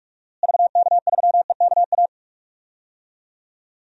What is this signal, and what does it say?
Morse code, heard as a keyed tone, about 48 wpm, 700 Hz. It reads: VK3EKA
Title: DXCC-100 Call Signs - 6 Characters - Repeated 2x - 45wpm